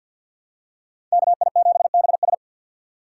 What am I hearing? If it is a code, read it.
Morse code, heard as a keyed tone, about 47 wpm, 700 Hz. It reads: KI7BS
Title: USA Call Signs - 5 Characters - 4x Character Spacing - 50wpm